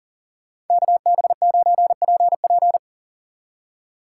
Morse code, heard as a keyed tone, about 49 wpm, 700 Hz. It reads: KB9PP
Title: Speed Racing - USA 2x2 Call Signs 40wpm